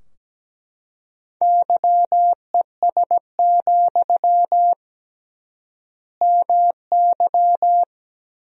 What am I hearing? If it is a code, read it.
Morse code, heard as a keyed tone, about 17 wpm, 700 Hz. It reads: YES, MY
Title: 3x Word-Spacing - Sentences from Top 500 Words - 17wpm